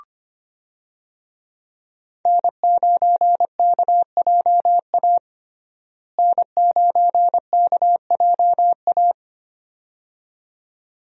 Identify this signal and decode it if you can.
Morse code, heard as a keyed tone, about 25 wpm, 700 Hz. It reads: N9KJA N9KJA
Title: USA Call Signs - 5 Characters - Repeated 2x - 25wpm